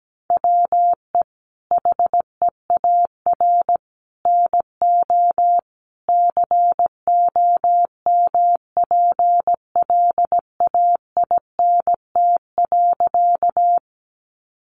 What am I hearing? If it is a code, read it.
Morse code, heard as a keyed tone, about 17 wpm, 700 Hz. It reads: WE HEAR NO COMPLAINT.
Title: Sentences from Top 1400 Words 17wpm